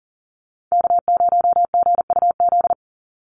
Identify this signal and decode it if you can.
Morse code, heard as a keyed tone, about 41 wpm, 700 Hz. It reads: K0GUZ
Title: Call Signs - 40wpm